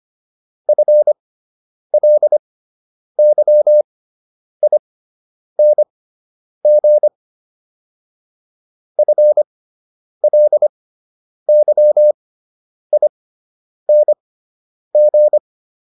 Morse code, heard as a keyed tone, about 25 wpm, spaced as for 9 wpm, 600 Hz. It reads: FLYING FLYING
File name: ING6 09